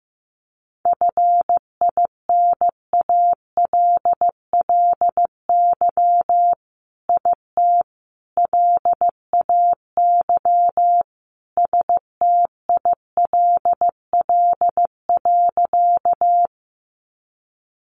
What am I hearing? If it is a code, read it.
Morse code, heard as a keyed tone, about 15 wpm, 700 Hz. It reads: FINALLY IT LAY STILL.
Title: Sentences from Top 1000 Words 15wpm